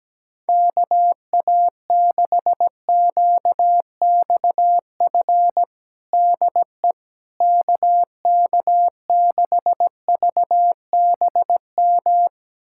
Morse code, heard as a keyed tone, about 17 wpm, 700 Hz. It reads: KA6QXF DE KK6VBM
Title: USA Callsign DE Callsign - 6 Characters - 17wpm